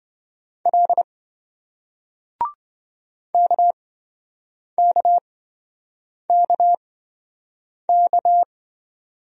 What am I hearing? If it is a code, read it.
Morse code, heard as a keyed tone, about 27 wpm, spaced as for 7 wpm, 700 Hz. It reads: L KKKK